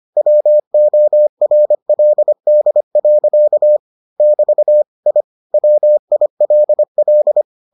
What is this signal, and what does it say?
Morse code, heard as a keyed tone, about 25 wpm, 600 Hz. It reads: WORLD. <BT> I WILL